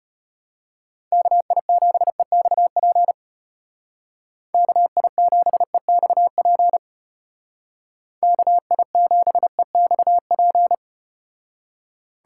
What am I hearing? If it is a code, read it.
Morse code, heard as a keyed tone, about 34 wpm, 700 Hz. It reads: KI7EXP KI7EXP KI7EXP